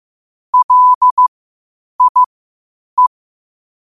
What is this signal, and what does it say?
Morse code, heard as a keyed tone, about 15 wpm, spaced as for 8 wpm, 1000 Hz. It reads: LIE
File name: qotdfiles / 8wpm